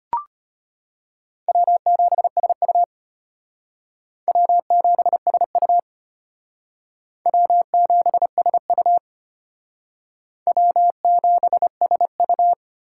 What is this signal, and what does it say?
Morse code, heard as a keyed tone, about 34 wpm, 700 Hz. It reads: W7SU W7SU W7SU W7SU